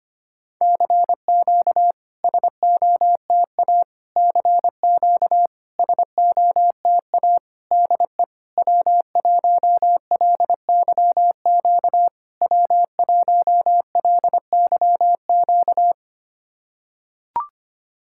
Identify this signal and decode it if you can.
Morse code, heard as a keyed tone, about 26 wpm, 700 Hz. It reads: CQ SOTA CQ SOTA DE W1LYQ W1LYQ
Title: SOTA QSO Practice 25wpm